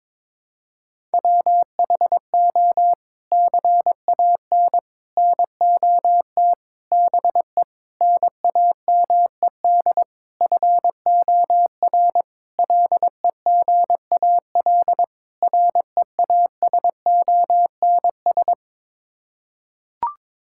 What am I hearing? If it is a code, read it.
Morse code, heard as a keyed tone, about 21 wpm, 700 Hz. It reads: WHO CAN NOT BE NAMED FOR LEGAL REASONS
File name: practice / 22wpm / Sets of 8 Words - 22wpm